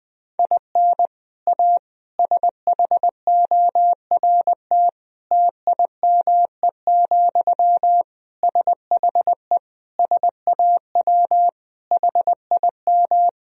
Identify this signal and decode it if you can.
Morse code, heard as a keyed tone, about 20 wpm, 700 Hz. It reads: IN A SHORT TIME, SHE SAW HIM
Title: Sentences from Top 1000 Words - Review Part 2 - 20wpm